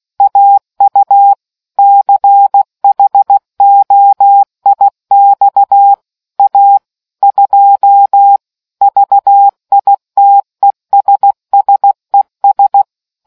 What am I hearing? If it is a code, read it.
Morse code, heard as a keyed tone, about 16 wpm, 800 Hz. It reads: AU CHOIX A 2 VITESSES